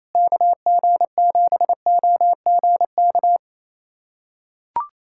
Morse code, heard as a keyed tone, about 28 wpm, 700 Hz. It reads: KG7OGK
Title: USA Callsign DE Callsign 28wpm